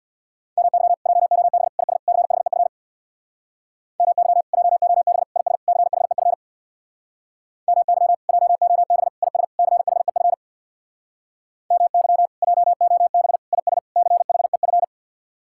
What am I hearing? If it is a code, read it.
Morse code, heard as a keyed tone, about 64 wpm, 700 Hz. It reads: MY JOB IS CHEF MY JOB IS CHEF MY JOB IS CHEF MY JOB IS CHEF